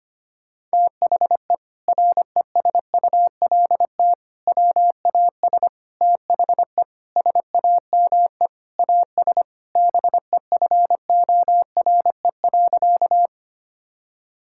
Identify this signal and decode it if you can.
Morse code, heard as a keyed tone, about 25 wpm, 700 Hz. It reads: THE RESULT WAS THE SAME AS BEFORE.
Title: Sentences from Top 1000 Words - Review Part 5 - 25wpm